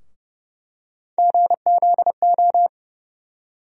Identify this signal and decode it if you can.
Morse code, heard as a keyed tone, about 30 wpm, 700 Hz. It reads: GZO
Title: USA Call Sign Postfixes - 3 Letters 30wpm